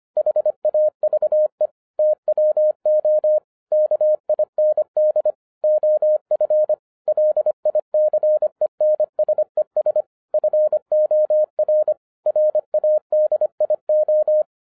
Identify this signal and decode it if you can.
Morse code, heard as a keyed tone, about 25 wpm, 610 Hz. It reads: HAVE TWO KIND OF LICENSES FOR RADIO